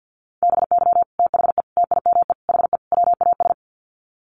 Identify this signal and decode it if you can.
Morse code, heard as a keyed tone, about 50 wpm, 700 Hz. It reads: BY THE TIME HE WAS
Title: Sets of 5 Words Encore 50wpm